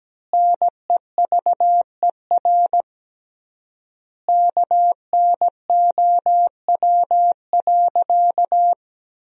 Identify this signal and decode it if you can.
Morse code, heard as a keyed tone, about 17 wpm, 700 Hz. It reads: NEVER KNOW.
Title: 3x Word-Spacing - Sentences from Top 300 Words - 17wpm